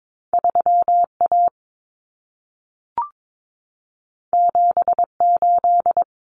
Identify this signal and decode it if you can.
Morse code, heard as a keyed tone, about 22 wpm, 700 Hz. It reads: 3A 78
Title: Common Cut Numbers - 2-Digit 22wpm